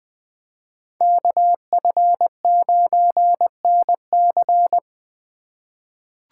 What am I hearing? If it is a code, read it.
Morse code, heard as a keyed tone, about 20 wpm, 700 Hz. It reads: KF9NC
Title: USA Call Signs - 5 Characters - 4x Character Spacing - 20wpm